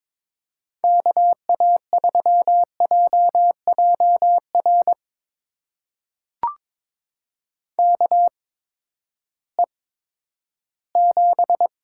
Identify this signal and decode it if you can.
Morse code, heard as a keyed tone, about 22 wpm, spaced as for 12 wpm, 700 Hz. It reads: KA3JJR KE7
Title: USA Call Signs - 6 Characters - 8x Character Spacing - 22wpm